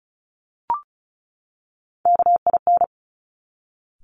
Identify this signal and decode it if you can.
Morse code, heard as a keyed tone, about 36 wpm, 690 Hz. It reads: KIN
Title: Top Three Letter Combinations in Words 35wpm